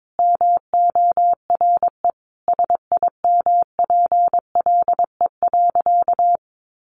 Morse code, heard as a keyed tone, about 22 wpm, 700 Hz. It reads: MORE SIMPLE.
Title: Sentences from Top 500 Words 22wpm